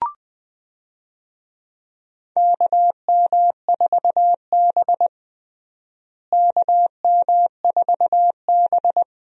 Morse code, heard as a keyed tone, about 19 wpm, 700 Hz. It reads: KM4B KM4B
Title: USA Call Signs - 4 Characters - Repeated 3x - 20wpm